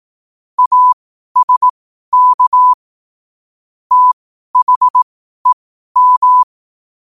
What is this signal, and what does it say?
Morse code, heard as a keyed tone, about 18 wpm, spaced as for 12 wpm, 1000 Hz. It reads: ASK THEM
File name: qotdfiles / 12wpm